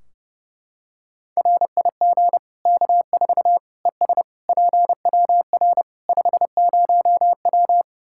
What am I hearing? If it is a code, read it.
Morse code, heard as a keyed tone, about 30 wpm, 700 Hz. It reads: RIG K4 ES PWR 50W
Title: K1USN SST QSO Practice 30wpm